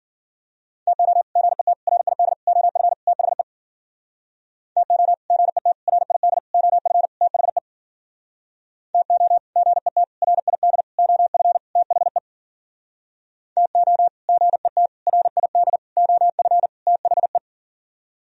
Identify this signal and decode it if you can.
Morse code, heard as a keyed tone, about 52 wpm, 700 Hz. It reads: TO GET RID OF THE TO GET RID OF THE TO GET RID OF THE TO GET RID OF THE